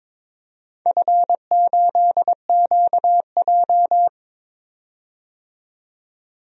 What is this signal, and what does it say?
Morse code, heard as a keyed tone, about 22 wpm, 700 Hz. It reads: F8QJ
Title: DXCC-100 Call Signs - 4 Characters - 22wpm